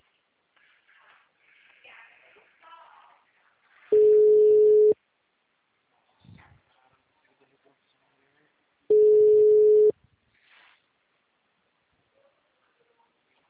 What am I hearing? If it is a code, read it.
Ringback tone (Europe, 425 Hz), with speech.
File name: out- -2004- - - .37869